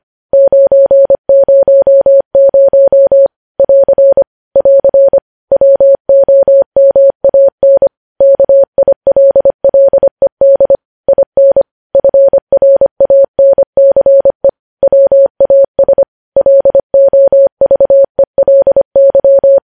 Morse code, heard as a keyed tone, about 25 wpm, 550 Hz. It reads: 900 <AR> <AR> WOMAN KILLED IN FRANCE WAS LOVELY
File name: podcast / News headlines-1x-25x25